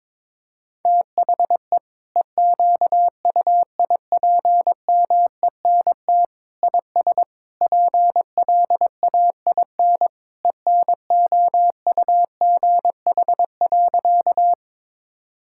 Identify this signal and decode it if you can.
Morse code, heard as a keyed tone, about 22 wpm, 700 Hz. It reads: THE EQUIPMENT IS PLAIN ENOUGH.
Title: Sentences from Top 1000 Words - Review Part 5 - 22wpm